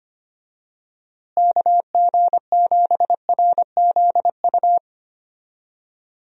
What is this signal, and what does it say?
Morse code, heard as a keyed tone, about 25 wpm, 700 Hz. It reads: KG7RZU
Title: USA Call Signs - 6 Characters - Repeated 3x - 25wpm